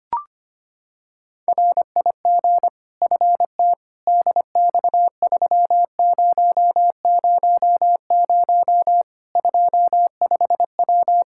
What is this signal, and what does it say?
Morse code, heard as a keyed tone, about 25 wpm, 700 Hz. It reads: RIG FT DX3000 25W
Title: Practice QSO - Exchange 2 - Info - 25wpm